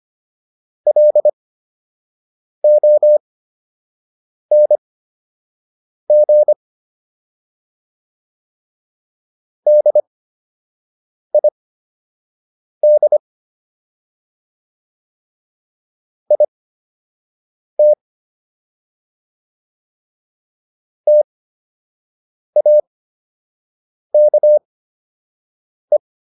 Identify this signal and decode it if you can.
Morse code, heard as a keyed tone, about 25 wpm, spaced as for 6 wpm, 600 Hz. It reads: LONG DID IT TAKE